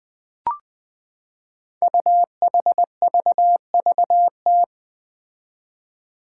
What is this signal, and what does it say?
Morse code, heard as a keyed tone, about 20 wpm, 700 Hz. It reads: UHVVT
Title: Five Challenging Letter-Number Combinations 20wpm